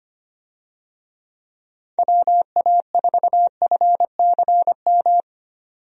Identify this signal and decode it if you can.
Morse code, heard as a keyed tone, about 25 wpm, 700 Hz. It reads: WA4FCM
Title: USA Call Signs - 6 Characters - 25wpm